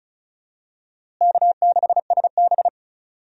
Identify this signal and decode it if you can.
Morse code, heard as a keyed tone, about 36 wpm, 700 Hz. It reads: K6SB
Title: USA Call Signs - 4 Characters - Repeated 3x - 35wpm